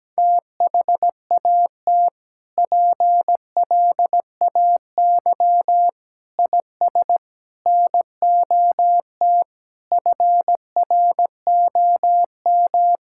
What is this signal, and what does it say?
Morse code, heard as a keyed tone, about 17 wpm, 700 Hz. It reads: THAT PLAY IS NOT FROM
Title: Sentences from Top 1000 Words - Review Part 5 - 17wpm